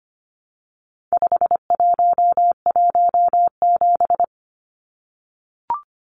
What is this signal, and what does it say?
Morse code, heard as a keyed tone, about 25 wpm, 700 Hz. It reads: 5117
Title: Numbers - 4-Digit 25wpm